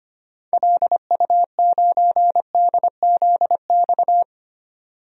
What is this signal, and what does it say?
Morse code, heard as a keyed tone, about 25 wpm, 700 Hz. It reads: LU9DZX
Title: DXCC-100 Call Signs - 6 Characters - 25wpm